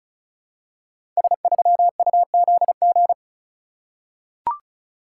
Morse code, heard as a keyed tone, about 36 wpm, 700 Hz. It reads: S3UZG
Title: Code Groups - 5 Letter:Numbers - 35wpm